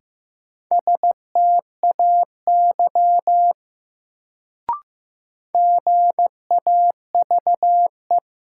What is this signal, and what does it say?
Morse code, heard as a keyed tone, about 15 wpm, 700 Hz. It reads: STAY GAVE
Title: Sentences from Top 500 Words - New Word Drill - 15wpm